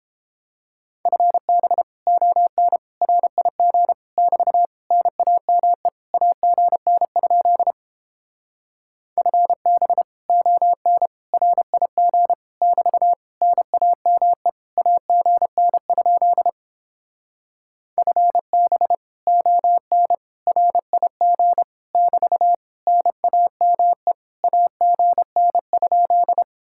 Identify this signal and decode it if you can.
Morse code, heard as a keyed tone, about 31 wpm, 700 Hz. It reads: FB ON RIG <BT> NAME AGN? FB ON RIG <BT> NAME AGN? FB ON RIG <BT> NAME AGN?